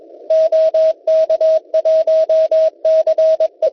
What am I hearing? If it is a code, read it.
Morse code, heard as a keyed tone, about 22 wpm, 640 Hz. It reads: OK1CE
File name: Test Pixid 160m / kiwi.oe9.at 1826.00 cw